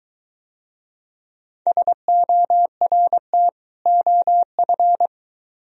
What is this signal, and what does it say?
Morse code, heard as a keyed tone, about 22 wpm, 700 Hz. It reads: SORT OF